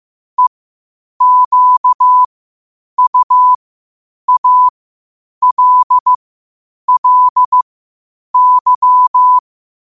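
Morse code, heard as a keyed tone, about 15 wpm, spaced as for 8 wpm, 1000 Hz. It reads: EQUALLY
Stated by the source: Morse code at 8 wpm.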